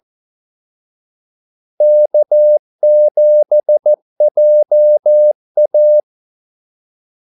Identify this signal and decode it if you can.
Morse code, heard as a keyed tone, about 14 wpm, 600 Hz. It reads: K7JA
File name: Callsigns-14wpm